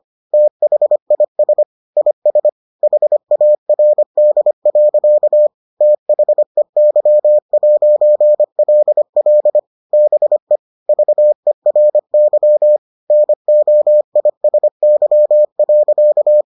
Morse code, heard as a keyed tone, about 25 wpm, 600 Hz. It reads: THIS IS HARD. THEY'LL BE VERY NOISY.